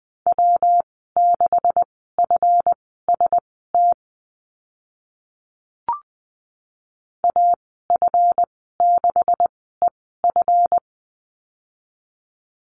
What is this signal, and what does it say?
Morse code, heard as a keyed tone, about 20 wpm, spaced as for 14 wpm, 700 Hz. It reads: W6FST AF6EF
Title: 147 Lesson 24g - Letter y - Call Signs 20wpm